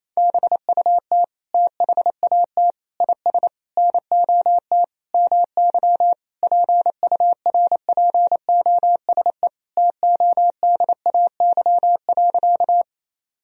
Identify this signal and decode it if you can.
Morse code, heard as a keyed tone, about 28 wpm, 700 Hz. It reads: BUT THAT IS NOT MY PURPOSE TODAY.
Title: Sentences from Top 1000 Words - Review Part 4 - 28wpm